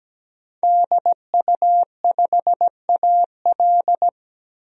Morse code, heard as a keyed tone, about 17 wpm, 700 Hz. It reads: DU5AL
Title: DXCC-100 Call Signs - 5 Characters - 17wpm